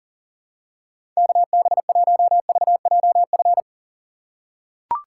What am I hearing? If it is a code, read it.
Morse code, heard as a keyed tone, about 40 wpm, 700 Hz. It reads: KB1VJF
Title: USA Call Signs - 6 Characters - 2x Character Spacing - 40wpm